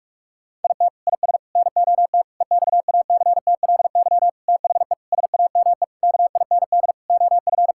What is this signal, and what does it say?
Morse code, heard as a keyed tone, about 46 wpm, 700 Hz. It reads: IT IS NOT EXACTLY THE SAME KIND OF
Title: Sentences from Top 1000 Words - Review Part 6 - 45wpm